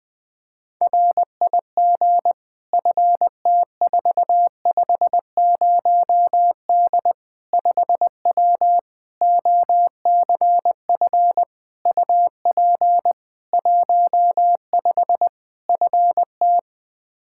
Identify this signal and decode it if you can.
Morse code, heard as a keyed tone, about 21 wpm, 700 Hz. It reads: RIG FT450D 5W OCF UP 15 FT
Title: K1USN SST QSO Practice 20wpm